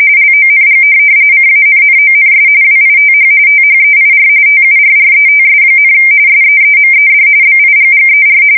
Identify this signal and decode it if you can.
RTTY, about 45.45 baud. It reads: LONG CANNONS WITH THEIR TURRETS, SEEMED BIGGER
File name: fsk170 / USS-New-Jersey